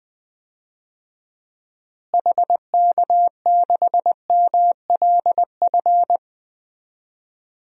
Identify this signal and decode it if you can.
Morse code, heard as a keyed tone, about 20 wpm, 700 Hz. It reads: HK6MLF
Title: DXCC-100 Call Signs - 6 Characters - Repeated 2x - 20wpm